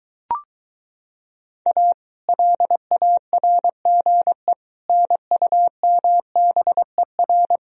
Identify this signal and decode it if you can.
Morse code, heard as a keyed tone, about 23 wpm, 700 Hz. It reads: A LARGE NUMBER